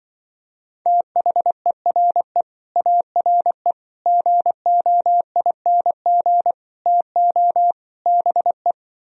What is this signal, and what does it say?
Morse code, heard as a keyed tone, about 23 wpm, 700 Hz. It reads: THERE ARE GOING TO BE
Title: Speed-Racing - Sets of 5 Words 20wpm